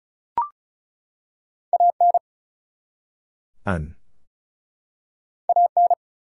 Morse code, heard as a keyed tone, about 35 wpm, 700 Hz. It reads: AN AN
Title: Two Letter Words - 35wpm